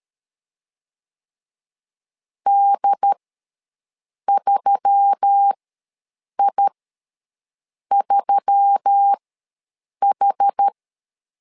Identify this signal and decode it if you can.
Morse code, heard as a keyed tone, about 13 wpm, spaced as for 6 wpm, 800 Hz. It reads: D3I3H
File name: LEZIONI / 4 Q1S3 ws 12wpm 2038